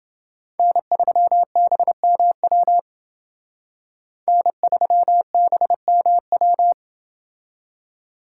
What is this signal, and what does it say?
Morse code, heard as a keyed tone, about 28 wpm, 700 Hz. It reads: N3BMW N3BMW